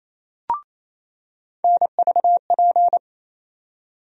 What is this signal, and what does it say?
Morse code, heard as a keyed tone, about 28 wpm, 700 Hz. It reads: NVP
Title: USA Call Sign Postfixes - 3 Letters 28wpm